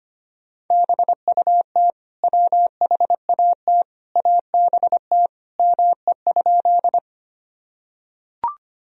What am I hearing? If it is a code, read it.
Morse code, heard as a keyed tone, about 25 wpm, 700 Hz. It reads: BUT WHAT ABT ME?
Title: Sentences from Top 200 Words and Abbreviations 25wpm